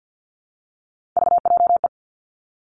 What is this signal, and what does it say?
Morse code, heard as a keyed tone, about 50 wpm, 700 Hz. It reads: VJE
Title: Three Challenging Letter-Number Combinations 50wpm